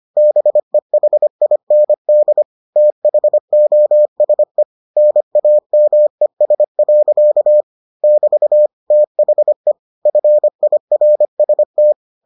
Morse code, heard as a keyed tone, about 25 wpm, 600 Hz. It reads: BEHIND THOSE NAMES. <BT> THE FIRST